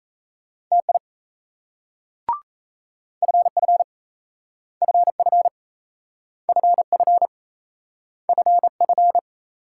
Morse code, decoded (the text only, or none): TI FF FF FF FF